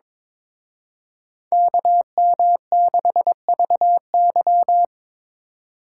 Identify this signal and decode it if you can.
Morse code, heard as a keyed tone, about 26 wpm, 700 Hz. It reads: KM6VY
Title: Speed Racing - USA Call Signs 22wpm